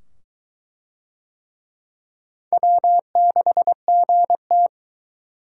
Morse code, heard as a keyed tone, about 22 wpm, 700 Hz. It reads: W6GT